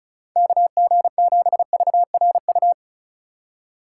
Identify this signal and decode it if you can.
Morse code, heard as a keyed tone, about 45 wpm, 700 Hz. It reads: KG7VRU